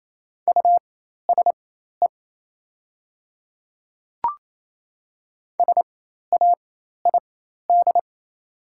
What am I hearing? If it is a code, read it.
Morse code, heard as a keyed tone, about 28 wpm, spaced as for 13 wpm, 700 Hz. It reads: USE SAID
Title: 4x Character Spacing - Top 100 Words - 28wpm